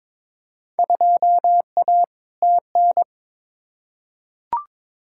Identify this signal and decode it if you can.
Morse code, heard as a keyed tone, about 22 wpm, 700 Hz. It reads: 2A TN
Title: Field Day Exchanges Encore - 22wpm